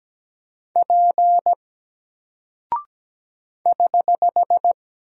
Morse code, heard as a keyed tone, about 17 wpm, 700 Hz. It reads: P <HH>
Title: Single Character (Letter:Number:Punctuation:Prosign) 17wpm